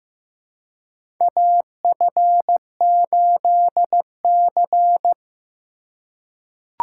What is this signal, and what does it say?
Morse code, heard as a keyed tone, about 15 wpm, 700 Hz. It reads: AF8C
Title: USA Call Signs - 4 Characters - 4x Character Spacing - 15wpm